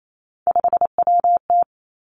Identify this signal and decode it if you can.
Morse code, heard as a keyed tone, about 28 wpm, 700 Hz. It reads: 5WT
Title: Three Challenging Letter-Number Combinations 28wpm